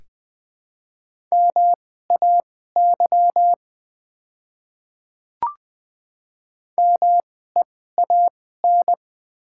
Morse code, heard as a keyed tone, about 21 wpm, spaced as for 15 wpm, 700 Hz. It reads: MAY MEAN
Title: 2x Character Spacing - Top 300 Words - 20wpm